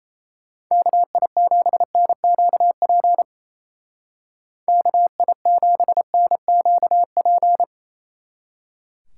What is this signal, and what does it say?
Morse code, heard as a keyed tone, about 35 wpm, 700 Hz. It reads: KI7NQP KI7NQP